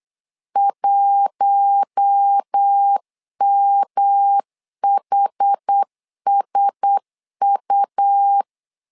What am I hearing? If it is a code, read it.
Morse code, heard as a keyed tone, about 8 wpm, 800 Hz. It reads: 1MHSU
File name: LEZIONI / 4 Q1S 8wpm 2710